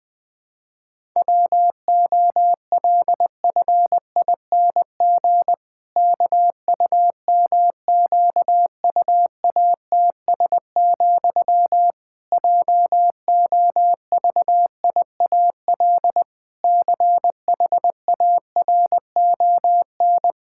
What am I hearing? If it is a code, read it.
Morse code, heard as a keyed tone, about 20 wpm, 700 Hz. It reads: WOLFING KUMQUATS, JOVIAL CHARON
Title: Pangram Challenge v2 - 20wpm